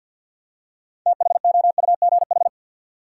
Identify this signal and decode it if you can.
Morse code, heard as a keyed tone, about 52 wpm, 700 Hz. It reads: THOUGH
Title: Six Letter Words - 50wpm